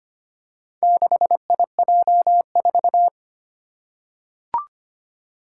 Morse code, heard as a keyed tone, about 25 wpm, 700 Hz. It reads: 6IJ4
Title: Four Challenging Letter-Number Combinations 25wpm